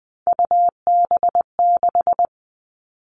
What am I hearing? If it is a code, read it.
Morse code, heard as a keyed tone, about 20 wpm, 700 Hz. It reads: UB6
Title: All Cut Numbers - 3-Digit 20wpm